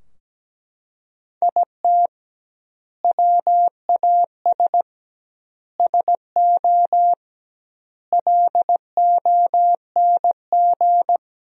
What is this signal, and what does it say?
Morse code, heard as a keyed tone, about 17 wpm, 700 Hz. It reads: IT WAS SO LONG
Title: 2x Word-Spacing - Sentences from Top 500 Words - 17wpm